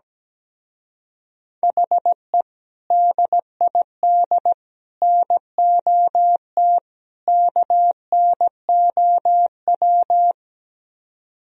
Morse code, heard as a keyed tone, about 17 wpm, 700 Hz. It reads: HE DID NOT KNOW
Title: Speed-Racing - Sets of 4 Words 17wpm